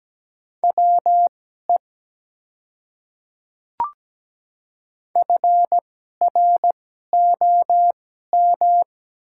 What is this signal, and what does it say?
Morse code, heard as a keyed tone, about 17 wpm, spaced as for 12 wpm, 700 Hz. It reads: WE FROM
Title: 2x Character Spacing - Top 100 Words - 17wpm